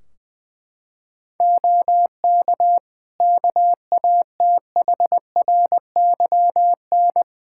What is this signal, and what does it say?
Morse code, heard as a keyed tone, about 21 wpm, 700 Hz. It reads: OK KATHRYN
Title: K1USN SST QSO Practice 20wpm